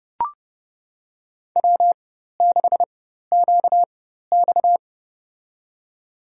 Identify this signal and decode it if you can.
Morse code, heard as a keyed tone, about 30 wpm, spaced as for 14 wpm, 700 Hz. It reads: W6QX
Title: USA Call Signs - 4 Characters - 4x Character Spacing - 30wpm